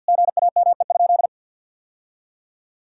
Morse code, heard as a keyed tone, about 50 wpm, 700 Hz. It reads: GAME?
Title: Sentences from Top 1600 Words 50wpm